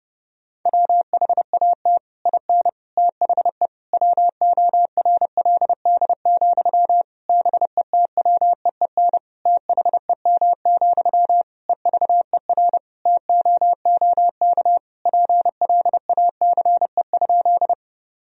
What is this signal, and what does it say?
Morse code, heard as a keyed tone, about 32 wpm, 700 Hz. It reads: WHAT IN THE WORLD, BETWEEN THEM, EVER TOOK PLACE?
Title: Sentences from Top 1000 Words - Review Part 1 - 30wpm